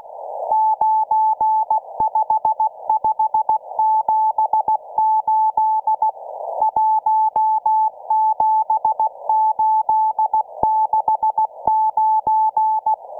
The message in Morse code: 95578 17869